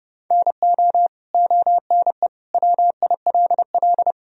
Morse code, heard as a keyed tone, about 32 wpm, 700 Hz. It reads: NO ONE WILL
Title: Sentences from Top 100 Words and Abbreviations 30wpm